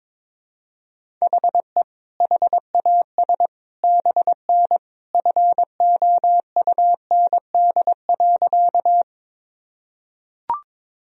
Morse code, heard as a keyed tone, about 22 wpm, 700 Hz. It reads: HE HAS BN FOUND.
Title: Sentences from Top 300 Words and Abbreviations 22wpm